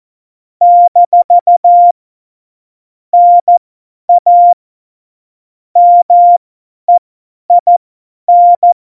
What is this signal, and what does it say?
Morse code, heard as a keyed tone, about 14 wpm, spaced as for 10 wpm, 700 Hz. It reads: - NA MEIN
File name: Morsekurs / Witze / Witz 05. Dezember 2017, 50 F70